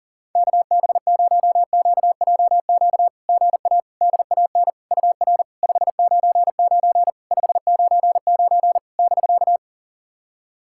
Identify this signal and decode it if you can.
Morse code, heard as a keyed tone, about 40 wpm, 700 Hz. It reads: KB0QJQ GA DAN UR 599 599 <BK>
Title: SOTA QSO Practice 40wpm